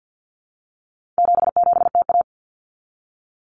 Morse code, heard as a keyed tone, about 50 wpm, 700 Hz. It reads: 77TA
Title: Common Cut Numbers - 4-Digit 50wpm